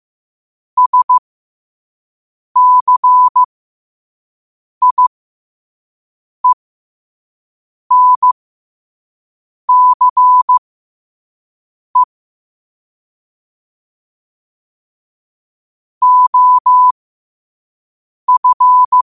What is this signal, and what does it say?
Morse code, heard as a keyed tone, about 15 wpm, spaced as for 5 wpm, 1000 Hz. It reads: SCIENCE OF
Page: New quotes every day in morse code at 5 Words per minute.